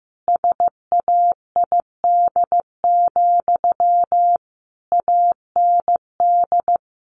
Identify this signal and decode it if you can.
Morse code, heard as a keyed tone, about 15 wpm, 700 Hz. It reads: SAID, AND
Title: Sentences from Top 300 Words 15wpm